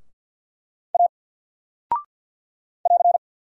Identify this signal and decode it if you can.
Morse code, heard as a keyed tone, about 48 wpm, 700 Hz. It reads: A <AR>
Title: Single Character (Letter:Number:Punctuation:Prosign) Alt 50wpm